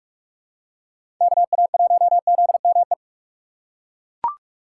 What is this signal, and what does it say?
Morse code, heard as a keyed tone, about 41 wpm, 700 Hz. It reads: KA1ZME